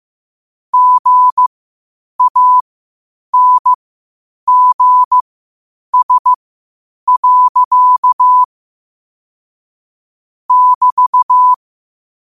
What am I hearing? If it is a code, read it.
Morse code, heard as a keyed tone, about 15 wpm, spaced as for 8 wpm, 1000 Hz. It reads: GANGS. <BT>
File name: qotdfiles / 8wpm